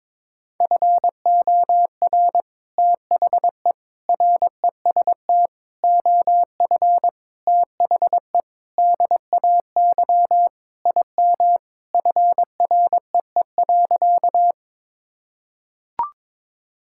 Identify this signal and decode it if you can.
Morse code, heard as a keyed tone, about 22 wpm, 700 Hz. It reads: FOR THE REST OF THE DAY IM FREE.
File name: practice / 22wpm / Sentences with 3-Word Phrases - 22wpm